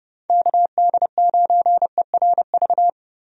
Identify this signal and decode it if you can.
Morse code, heard as a keyed tone, about 30 wpm, 700 Hz. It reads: KD9ERV
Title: USA Call Signs - 6 Characters - Repeated 3x - 30wpm